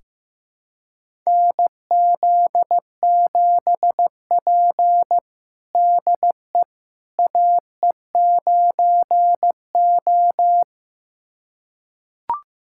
Morse code, heard as a keyed tone, about 15 wpm, 700 Hz. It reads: NZ7P DE AE9O
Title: USA Callsign DE Callsign - 4 Characters - 15wpm